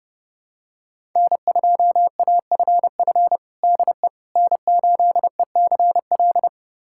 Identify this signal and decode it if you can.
Morse code, heard as a keyed tone, about 30 wpm, 700 Hz. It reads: N2AFF DE N8ECL
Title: USA Callsign DE Callsign - 5 Characters - 30wpm